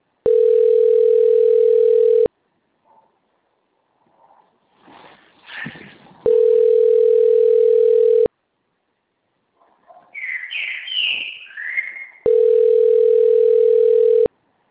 Ringback tone (North America), 440 + 480 Hz.
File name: rg-1600-unknown- - - .63804